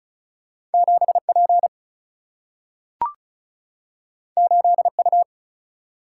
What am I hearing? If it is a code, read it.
Morse code, heard as a keyed tone, about 36 wpm, 700 Hz. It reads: 7P 8U
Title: Single-Letter Single-Number 35wpm